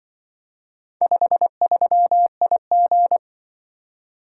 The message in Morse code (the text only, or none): 53IG